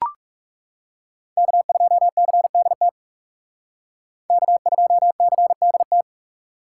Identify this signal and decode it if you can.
Morse code, heard as a keyed tone, about 36 wpm, 700 Hz. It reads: K2CDT K2CDT
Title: Speed Racing - USA 1x3 Call Signs 30wpm